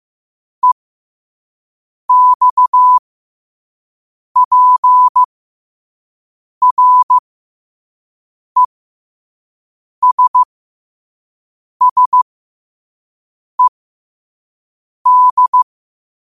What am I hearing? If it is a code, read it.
Morse code, heard as a keyed tone, about 15 wpm, spaced as for 5 wpm, 1000 Hz. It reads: EXPRESSED
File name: 5wpm